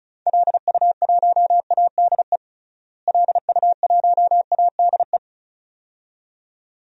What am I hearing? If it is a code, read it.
Morse code, heard as a keyed tone, about 34 wpm, 700 Hz. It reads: LU1ADE LU1ADE
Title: DXCC-100 Call Signs - 6 Characters - Repeated 2x - 35wpm